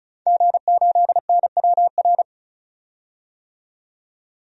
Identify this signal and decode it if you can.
Morse code, heard as a keyed tone, about 34 wpm, 700 Hz. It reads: G8NWR
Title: DXCC-100 Call Signs - 5 Characters - 35wpm